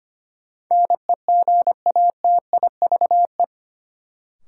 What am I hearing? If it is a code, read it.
Morse code, heard as a keyed tone, about 25 wpm, 700 Hz. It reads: NEGATIVE
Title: Sentences from Top 1000 Words - New Word Drill - 25wpm